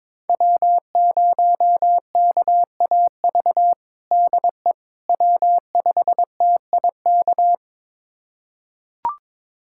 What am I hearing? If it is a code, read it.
Morse code, heard as a keyed tone, about 21 wpm, 700 Hz. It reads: W0KAV DE W5TIK
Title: USA Callsign DE Callsign - 5 Characters - 22wpm